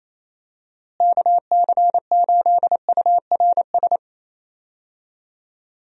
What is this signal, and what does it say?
Morse code, heard as a keyed tone, about 28 wpm, 700 Hz. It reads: KC8URS
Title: USA Call Signs - 6 Characters - Repeated 3x - 28wpm